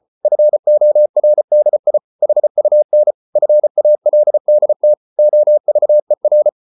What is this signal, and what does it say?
Morse code, heard as a keyed tone, about 34 wpm, 600 Hz. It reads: FORDI HUN FALDT OVER
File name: Ferie QRQ 34wpm